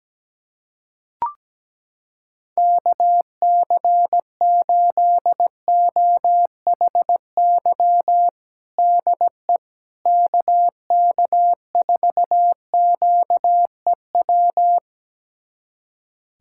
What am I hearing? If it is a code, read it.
Morse code, heard as a keyed tone, about 17 wpm, 700 Hz. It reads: KC8OHY DE KK4QEW
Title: USA Callsign DE Callsign 17wpm